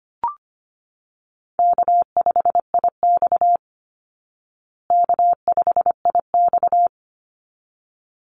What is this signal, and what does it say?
Morse code, heard as a keyed tone, about 25 wpm, 700 Hz. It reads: K5IX K5IX
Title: Easy Call Signs Repeated 2x - 25wpm